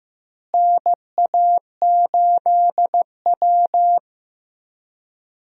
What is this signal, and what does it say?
Morse code, heard as a keyed tone, about 15 wpm, 700 Hz. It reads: NA8W
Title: USA Call Signs - 4 Characters - Repeated 3x - 15wpm